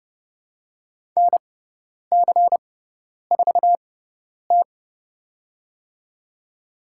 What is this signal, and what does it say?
Morse code, heard as a keyed tone, about 30 wpm, spaced as for 10 wpm, 700 Hz. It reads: NC4T
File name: Lesson 16g - Letter c - Call Signs 30wpm